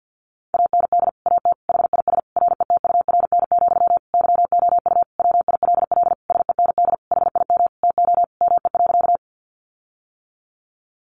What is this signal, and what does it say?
Morse code, heard as a keyed tone, about 50 wpm, 700 Hz. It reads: AND AT HIS RETURN, YOU WILL SEND HIM TO ME.
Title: Sentences from Top 1500 Words 50wpm